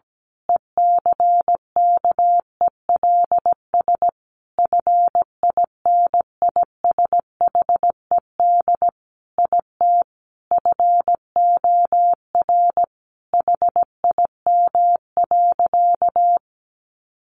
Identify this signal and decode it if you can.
Morse code, heard as a keyed tone, about 17 wpm, 700 Hz. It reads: ECKELS FINISHED IT FOR HIM.
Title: A Sound Of Thunder 17wpm